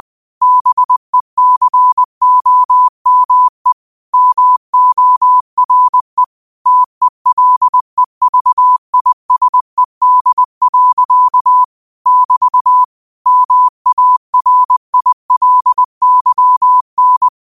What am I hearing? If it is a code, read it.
Morse code, heard as a keyed tone, about 20 wpm, 1000 Hz. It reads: BECOME MORE TELEVISED. <BT> MARILYN